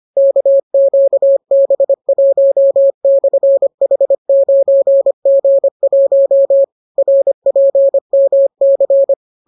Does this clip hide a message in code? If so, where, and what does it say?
Yes: Morse code from 0.2 to 9.1 s, heard as a keyed tone, about 25 wpm, 550 Hz. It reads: KQB1/H9G1 RPMC